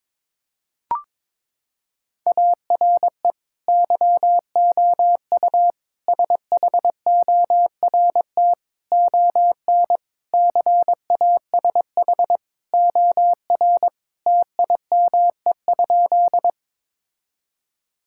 Morse code, heard as a keyed tone, about 22 wpm, 700 Hz. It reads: ARE YOU SHORT ON CASH OR TIME?
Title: Sentences from Top 1000 Words - Review Part 5 - 22wpm